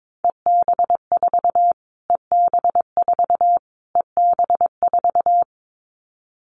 Morse code, heard as a keyed tone, about 22 wpm, 700 Hz. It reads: EB4 EB4 EB4
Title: Callsign Prefixes for Top 25 DXCC - Repeated 3x - 22wpm